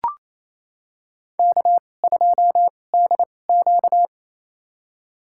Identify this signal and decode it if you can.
Morse code, heard as a keyed tone, about 28 wpm, spaced as for 20 wpm, 700 Hz. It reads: K2DQ
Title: USA Call Signs - 2x Character Spacing - 28wpm